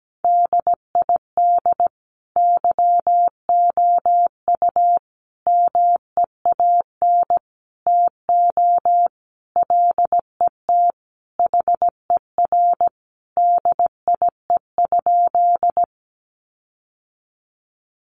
Morse code, heard as a keyed tone, about 17 wpm, 700 Hz. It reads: DID YOU MEAN TO LET HER DIE?
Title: Sentences from Top 1300 Words 17wpm